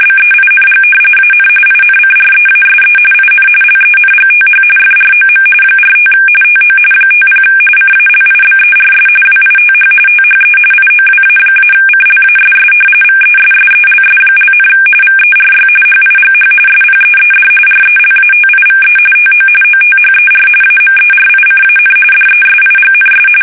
RTTY, about 45.45 baud. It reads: BEING AN INCREASE IN LENGTH FROM 45 TO 50 CALIBERS WITH THE IOWA-CLASS VESSELS), VERY HIGH SPEED RELATIVE TO OTHER AMERICAN DESIGNS
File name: fsk850 / battleships